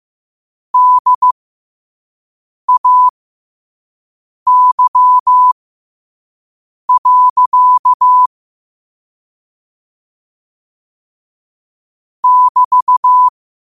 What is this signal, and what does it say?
Morse code, heard as a keyed tone, about 15 wpm, spaced as for 5 wpm, 1000 Hz. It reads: DAY. <BT>
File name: qotdfiles / 5wpm